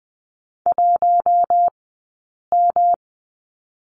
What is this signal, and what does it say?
Morse code, heard as a keyed tone, about 20 wpm, 700 Hz. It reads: 1 M
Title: 144 Lesson 24d - Letter y - Two Characters 20wpm